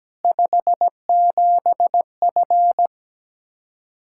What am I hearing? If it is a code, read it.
Morse code, heard as a keyed tone, about 17 wpm, 700 Hz. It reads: 57F
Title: Practice QSO - Exchange 2 - Info - 17wpm